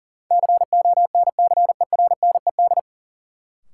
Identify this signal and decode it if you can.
Morse code, heard as a keyed tone, about 38 wpm, 700 Hz. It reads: CONCERNED
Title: Sentences from Top 1000 Words - Review Part 6 - New Word Drill - 40wpm